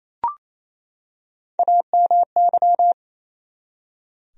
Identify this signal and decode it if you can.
Morse code, heard as a keyed tone, about 27 wpm, 700 Hz. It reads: AMY
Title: USA Common Names - Max 4 Letters - 28wpm